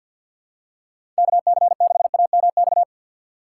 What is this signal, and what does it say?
Morse code, heard as a keyed tone, about 53 wpm, 690 Hz. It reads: KC6AMX
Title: USA Call Signs - 6 Characters - 8x Character Spacing - 50wpm